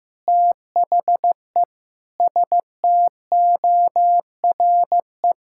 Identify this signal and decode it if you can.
Morse code, heard as a keyed tone, about 15 wpm, 700 Hz. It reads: THE STORE
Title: Sentences from Top 1000 Words - Review Part 5 - 15wpm